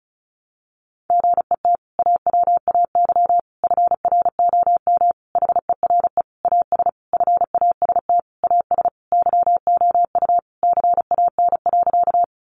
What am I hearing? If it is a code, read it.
Morse code, heard as a keyed tone, about 34 wpm, 700 Hz. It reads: GET AWAY FROM HERE AS FAST AS YOU CAN.
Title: Sentences from Top 600 Words 35wpm